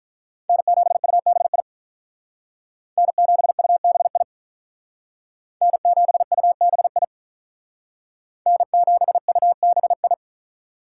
Morse code, heard as a keyed tone, about 45 wpm, 700 Hz. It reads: N7UBI N7UBI N7UBI N7UBI